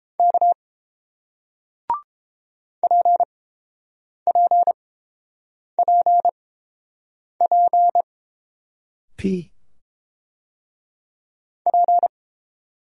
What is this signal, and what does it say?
Morse code, heard as a keyed tone, about 30 wpm, spaced as for 5 wpm, 700 Hz. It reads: K PPPPP